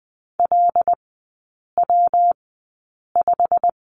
Morse code, heard as a keyed tone, about 20 wpm, spaced as for 8 wpm, 700 Hz. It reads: LW5
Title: 121 Lesson 20e - Number 3 - Three Characters 20wpm